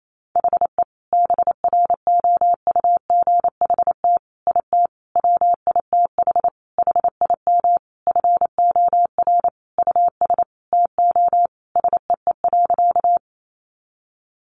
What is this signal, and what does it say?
Morse code, heard as a keyed tone, about 28 wpm, 700 Hz. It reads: HE BROUGHT IT WITH HIM FOR US TO SEE.
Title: Sentences from Top 500 Words 28wpm